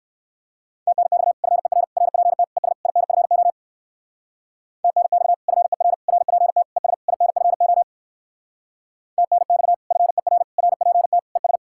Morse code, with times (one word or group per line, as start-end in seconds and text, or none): 0.9-1.3 TNX
1.4-1.8 FER
2.0-2.4 RPT
2.6-2.7 ES
2.9-3.5 INFO
4.8-5.3 TNX
5.5-5.9 FER
6.1-6.6 RPT
6.8-6.9 ES
7.1-7.8 INFO
9.2-9.7 TNX
9.9-10.4 FER
10.6-11.2 RPT
11.4-11.6 ES